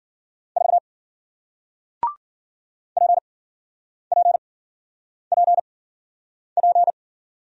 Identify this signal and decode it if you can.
Morse code, heard as a keyed tone, about 55 wpm, spaced as for 9 wpm, 700 Hz. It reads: 4 PPPP